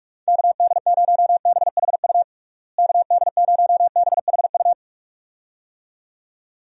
Morse code, heard as a keyed tone, about 41 wpm, 700 Hz. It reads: KD0BHU KD0BHU
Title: USA Call Signs - 6 Characters - Repeated 2x - 45wpm